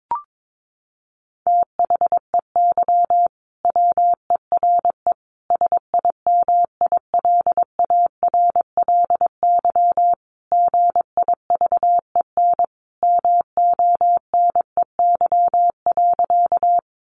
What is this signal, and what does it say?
Morse code, heard as a keyed tone, about 22 wpm, 700 Hz. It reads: THEY WERE SIMILARLY GIVEN MONEY.
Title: Sentences from Top 1400 Words 22wpm